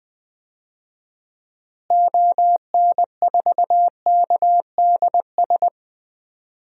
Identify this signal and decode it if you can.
Morse code, heard as a keyed tone, about 20 wpm, 700 Hz. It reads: ON4KDS
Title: DXCC-100 Call Signs - 6 Characters - Repeated 3x - 20wpm